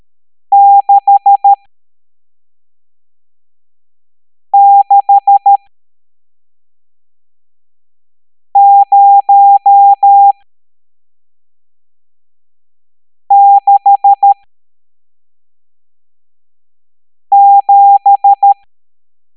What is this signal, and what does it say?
Morse code, heard as a keyed tone, about 13 wpm, spaced as for 3 wpm, 800 Hz. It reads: 66067
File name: hamdata / L11-EX01-67890